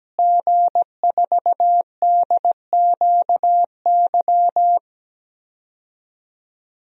Morse code, heard as a keyed tone, about 17 wpm, 700 Hz. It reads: G4DQY
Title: DXCC-100 Call Signs - 5 Characters - Repeated 2x - 17wpm